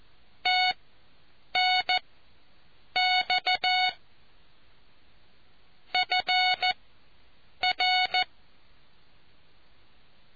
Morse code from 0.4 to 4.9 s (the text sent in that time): TNX